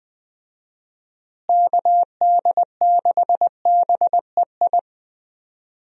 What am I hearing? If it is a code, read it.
Morse code, heard as a keyed tone, about 20 wpm, 700 Hz. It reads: KD6BEI
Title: USA Call Signs - 6 Characters - Repeated 3x - 20wpm